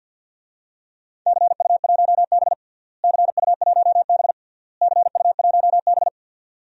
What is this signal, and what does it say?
Morse code, heard as a keyed tone, about 51 wpm, 700 Hz. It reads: CU1B CU1B CU1B
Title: DXCC-100 Call Signs - 4 Characters - Repeated 3x - 50wpm